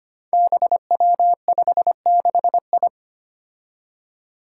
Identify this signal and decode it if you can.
Morse code, heard as a keyed tone, about 25 wpm, 700 Hz. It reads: BW56I
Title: Five Challenging Letter-Number Combinations 25wpm